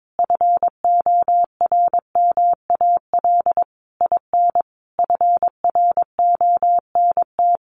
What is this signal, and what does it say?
Morse code, heard as a keyed tone, about 22 wpm, 700 Hz. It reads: FORMAL IN FRONT
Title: Sentences from Top 1300 Words 22wpm